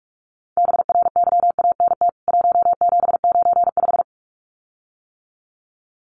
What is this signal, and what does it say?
Morse code, heard as a keyed tone, about 44 wpm, 700 Hz. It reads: BRYANT 1795
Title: CWT Exchange 45wpm